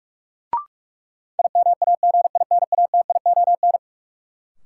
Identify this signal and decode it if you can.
Morse code, heard as a keyed tone, about 46 wpm, 700 Hz. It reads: IMAGINATION
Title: Sentences from Top 1300 Words - New Word Drill - 45wpm